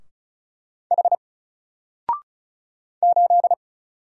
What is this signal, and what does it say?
Morse code, heard as a keyed tone, about 36 wpm, 700 Hz. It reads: H 8
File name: Single-Letter-Number-35wpm